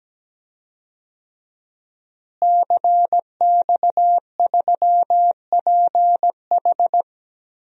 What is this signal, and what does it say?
Morse code, heard as a keyed tone, about 17 wpm, 700 Hz. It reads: CX3PH
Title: DXCC-100 Call Signs - 5 Characters - Repeated 3x - 17wpm